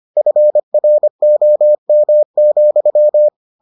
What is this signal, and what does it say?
Morse code, heard as a keyed tone, about 25 wpm, 600 Hz. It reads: FROM,